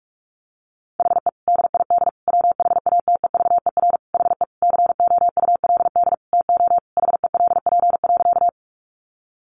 Morse code, heard as a keyed tone, about 44 wpm, 700 Hz. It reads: HE DID WHATEVER HE COULD TO HELP.
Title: Sentences from Top 1100 Words 45wpm